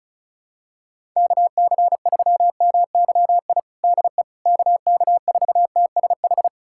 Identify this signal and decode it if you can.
Morse code, heard as a keyed tone, about 34 wpm, 700 Hz. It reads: KC3MYI DE KK4TSH
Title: USA Callsign DE Callsign - 6 Characters - 35wpm